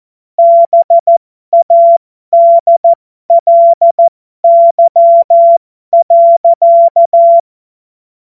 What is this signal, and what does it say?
Morse code, heard as a keyed tone, about 14 wpm, spaced as for 12 wpm, 680 Hz. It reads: BADLY.
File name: Morsekurs / Witze